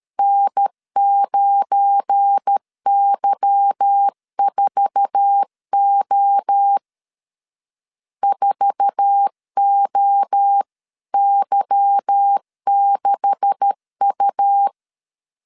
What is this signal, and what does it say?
Morse code, heard as a keyed tone, about 13 wpm, 800 Hz. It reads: N9Y4O 4OY6U
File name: CW / DET / 1 9 1500C S A 12wpm 3800